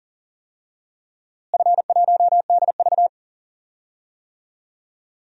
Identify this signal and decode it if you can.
Morse code, heard as a keyed tone, about 37 wpm, 700 Hz. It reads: F1DV
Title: DXCC-100 Call Signs - 4 Characters - Repeated 2x - 40wpm